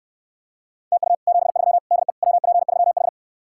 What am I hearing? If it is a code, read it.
Morse code, heard as a keyed tone, about 69 wpm, 700 Hz. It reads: TU 73 DE WP3H